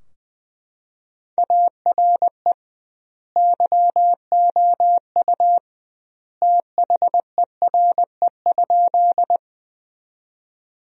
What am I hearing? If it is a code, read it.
Morse code, heard as a keyed tone, about 21 wpm, 700 Hz. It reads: ARE YOU THERE?
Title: 2x Word-Spacing - Sentences from Top 100 Words - 20wpm